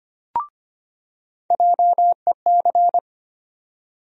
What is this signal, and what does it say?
Morse code, heard as a keyed tone, about 25 wpm, 700 Hz. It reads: JEC
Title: USA Call Sign Postfixes - 3 Letters 25wpm